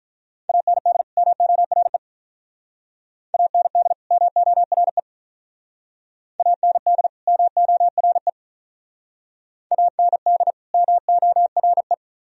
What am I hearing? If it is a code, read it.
Morse code, heard as a keyed tone, about 45 wpm, 700 Hz. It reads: AND MORE AND MORE AND MORE AND MORE